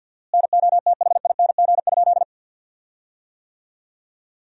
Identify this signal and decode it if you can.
Morse code, heard as a keyed tone, about 53 wpm, 700 Hz. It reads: NOTHING?
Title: 4x Word-Spacing - Sentences from Top 500 Words - 50wpm